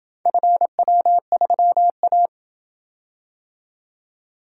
Morse code, heard as a keyed tone, about 27 wpm, 700 Hz. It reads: FW3A